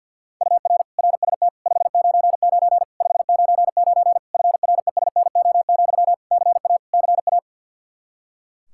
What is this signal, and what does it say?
Morse code, heard as a keyed tone, about 48 wpm, 700 Hz. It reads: UR RST 599 599 FRESNO, CA CA
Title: Practice QSO - Exchange 1 - Beginning - 50wpm